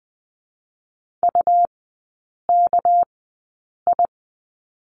Morse code, heard as a keyed tone, about 20 wpm, spaced as for 8 wpm, 690 Hz. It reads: UKI
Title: 205 Lesson 33e - Letter j - Three Characters 20wpm